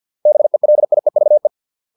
Morse code, heard as a keyed tone, about 50 wpm, 600 Hz. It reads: BELIEVE